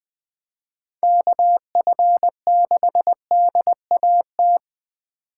Morse code, heard as a keyed tone, about 20 wpm, 700 Hz. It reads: KF6DAT
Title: USA Call Signs - 3x Character Spacing - 20wpm